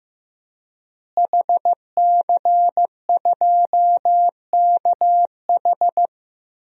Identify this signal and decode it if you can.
Morse code, heard as a keyed tone, about 15 wpm, 700 Hz. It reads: HC2KH
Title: DXCC-100 Call Signs - 5 Characters - Repeated 2x - 15wpm